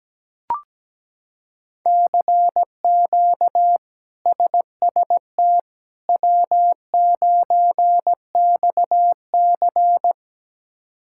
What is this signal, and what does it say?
Morse code, heard as a keyed tone, about 17 wpm, 700 Hz. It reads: CQ SST W9XC
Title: K1USN SST QSO Practice - 17wpm